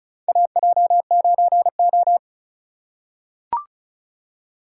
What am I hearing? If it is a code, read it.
Morse code, heard as a keyed tone, about 34 wpm, 700 Hz. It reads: AJ9O
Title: USA Call Signs - 4 Characters - 4x Character Spacing - 35wpm